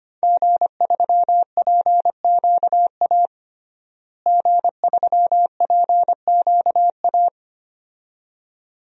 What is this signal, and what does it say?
Morse code, heard as a keyed tone, about 25 wpm, 700 Hz. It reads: G3PQA G3PQA
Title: DXCC-100 Call Signs - 5 Characters - Repeated 2x - 25wpm